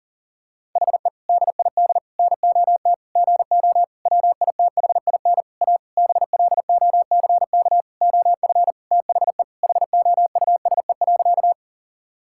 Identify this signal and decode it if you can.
Morse code, heard as a keyed tone, about 37 wpm, 700 Hz. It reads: HE DID NOT GO WITHIN A BLOCK OF THE HOUSE.
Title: Sentences from Top 1600 Words 40wpm